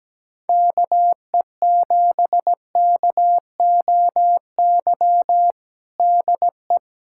Morse code, heard as a keyed tone, about 17 wpm, 700 Hz. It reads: KE7KOY DE
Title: USA Callsign DE Callsign - 6 Characters - 17wpm